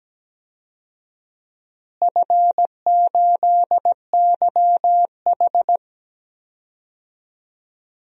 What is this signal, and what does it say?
Morse code, heard as a keyed tone, about 17 wpm, 700 Hz. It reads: F8YH
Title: DXCC-100 Call Signs - 4 Characters - Repeated 2x - 17wpm